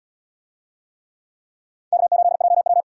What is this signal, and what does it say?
Morse code, heard as a keyed tone, about 69 wpm, 700 Hz. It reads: K8JW